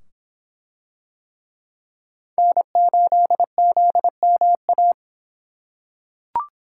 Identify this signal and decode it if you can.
Morse code, heard as a keyed tone, about 25 wpm, 700 Hz. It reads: N8ZMA